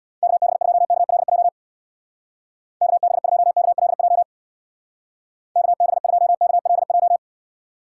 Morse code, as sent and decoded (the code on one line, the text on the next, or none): -.- -... ..--- -.- .-.. .--- / -.- -... ..--- -.- .-.. .--- / -.- -... ..--- -.- .-.. .---
KB2KLJ KB2KLJ KB2KLJ